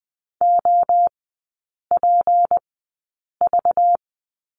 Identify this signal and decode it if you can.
Morse code, heard as a keyed tone, about 20 wpm, spaced as for 8 wpm, 700 Hz. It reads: OPV
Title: 191 Lesson 31e - Letter v - Three Characters 20wpm